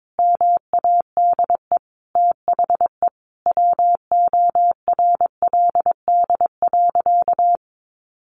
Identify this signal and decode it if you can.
Morse code, heard as a keyed tone, about 22 wpm, 700 Hz. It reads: MADE THE WORLD.
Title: Sentences from Top 200 Words 22wpm